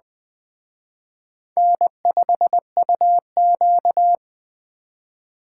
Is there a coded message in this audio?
Yes: Morse code from 1.6 to 4.2 s, heard as a keyed tone, about 19 wpm, 700 Hz. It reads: N5UQ